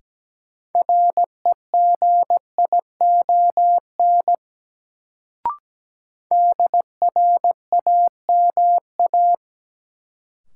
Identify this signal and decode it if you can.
Morse code, heard as a keyed tone, about 17 wpm, 700 Hz. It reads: REGION DRAMA
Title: Sentences from Top 1500 Words - New Word Drill Alt - 17wpm